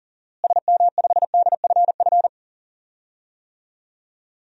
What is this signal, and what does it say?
Morse code, heard as a keyed tone, about 40 wpm, 700 Hz. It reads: SM5DFF
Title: DXCC-100 Call Signs - 6 Characters - Repeated 2x - 40wpm